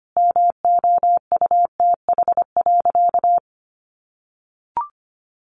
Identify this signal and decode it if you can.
Morse code, heard as a keyed tone, about 25 wpm, 700 Hz. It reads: MOUTH.
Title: A Sound Of Thunder 25wpm